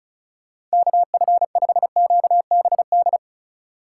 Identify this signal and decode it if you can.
Morse code, heard as a keyed tone, about 34 wpm, 700 Hz. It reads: KF5QBD
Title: USA Call Signs - 3x Character Spacing - 35wpm